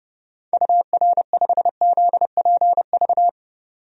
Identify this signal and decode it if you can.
Morse code, heard as a keyed tone, about 32 wpm, 700 Hz. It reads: UR5ZPV
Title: DXCC-100 Call Signs - 6 Characters - 30wpm